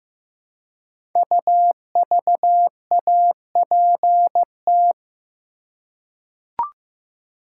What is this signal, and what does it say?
Morse code, heard as a keyed tone, about 15 wpm, 700 Hz. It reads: UVAPT
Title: Code Groups - 5 Letter:Numbers - 15wpm